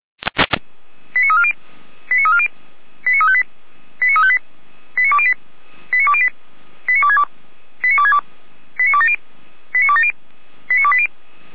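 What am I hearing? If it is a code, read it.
Five-tone selective call (ZVEI), heard as quick runs of five tones, tones of about 80 ms. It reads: 89340 89340 89358 89358 89208 89208 89262 89262 89270 89270 89280
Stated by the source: ZVEI five-tone selective call 89333 received on an SDR.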